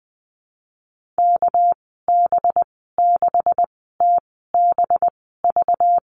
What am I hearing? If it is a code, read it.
Morse code, heard as a keyed tone, about 20 wpm, spaced as for 15 wpm, 700 Hz. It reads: KB6TBV
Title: 200 Lesson 32g - Letter k - Call Signs 20wpm